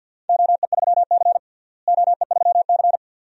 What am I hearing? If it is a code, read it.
Morse code, heard as a keyed tone, about 50 wpm, 700 Hz. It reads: OE3C OE3C
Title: DXCC-100 Call Signs - 4 Characters - Repeated 2x - 50wpm